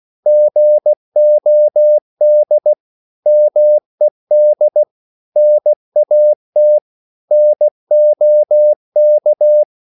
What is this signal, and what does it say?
Morse code, heard as a keyed tone, about 16 wpm, 600 Hz. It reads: GOD MED NAT NOK
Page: Korte danske ord 16wpm